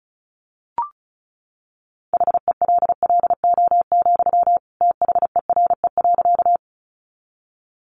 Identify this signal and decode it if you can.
Morse code, heard as a keyed tone, about 34 wpm, 700 Hz. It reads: HELLO, THERE.
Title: Sentences from Top 1100 Words 35wpm